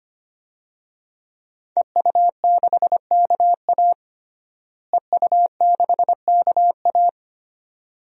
Morse code, heard as a keyed tone, about 25 wpm, 700 Hz. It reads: EU6KA EU6KA
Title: DXCC-100 Call Signs - 5 Characters - Repeated 2x - 25wpm